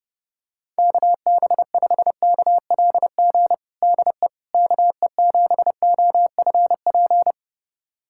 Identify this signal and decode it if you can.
Morse code, heard as a keyed tone, about 30 wpm, 700 Hz. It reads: KB5KLG DE KE7OFP
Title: USA Callsign DE Callsign 30wpm